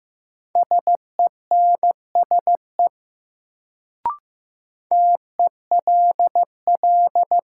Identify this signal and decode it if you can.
Morse code, heard as a keyed tone, about 15 wpm, 700 Hz. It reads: SENSE TELL
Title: Sets of 2 Words - New Word Drill - 15wpm